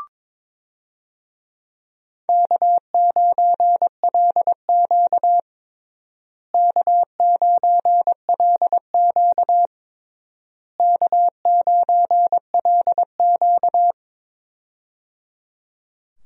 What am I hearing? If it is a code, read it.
Morse code, heard as a keyed tone, about 22 wpm, 700 Hz. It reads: K9LQ K9LQ K9LQ
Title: USA Call Signs - 4 Characters - Repeated 3x - 22wpm